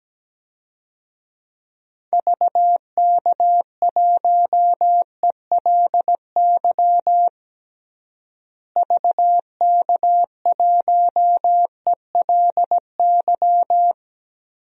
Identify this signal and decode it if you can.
Morse code, heard as a keyed tone, about 17 wpm, 700 Hz. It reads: VK1ELY VK1ELY
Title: DXCC-100 Call Signs - 6 Characters - Repeated 3x - 17wpm